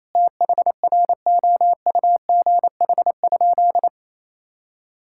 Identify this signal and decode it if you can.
Morse code, heard as a keyed tone, about 28 wpm, 700 Hz. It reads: THROUGH?
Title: Sentences from Top 1000 Words - Review Part 6 - 28wpm